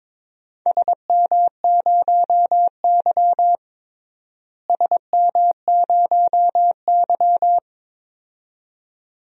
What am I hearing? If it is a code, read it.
Morse code, heard as a keyed tone, about 21 wpm, 700 Hz. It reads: SM0Y SM0Y
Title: DXCC-100 Call Signs - 4 Characters - Repeated 2x - 22wpm